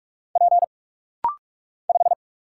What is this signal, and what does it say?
Morse code, heard as a keyed tone, about 44 wpm, 700 Hz. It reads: P 5
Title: Single Letter-Number - Rapid-Fire 45wpm